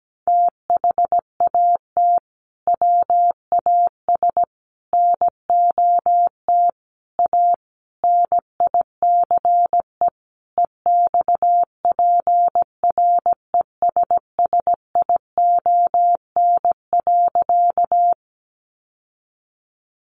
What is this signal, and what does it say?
Morse code, heard as a keyed tone, about 17 wpm, 700 Hz. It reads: THAT WAS NOT A NICE EXPRESSION.
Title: Sentences from Top 1200 Words 17wpm